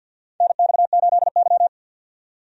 Morse code, heard as a keyed tone, about 51 wpm, 700 Hz. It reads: NX8Y
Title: USA Call Signs - 4 Characters - 50wpm